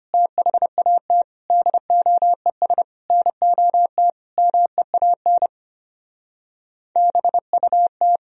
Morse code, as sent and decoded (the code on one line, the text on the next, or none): - .... .- - / -.. --- . ... / -. --- - / -- . .- -. / -... ..- -
THAT DOES NOT MEAN BUT